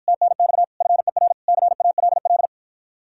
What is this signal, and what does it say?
Morse code, heard as a keyed tone, about 53 wpm, 700 Hz. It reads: TNX FER CALL